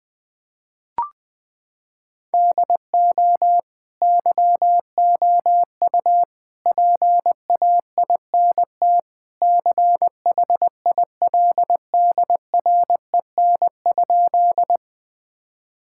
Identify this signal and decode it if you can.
Morse code, heard as a keyed tone, about 20 wpm, 700 Hz. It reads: DO YOU PAINT CHILDREN?
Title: Sentences from Top 1000 Words - Review Part 6 - 20wpm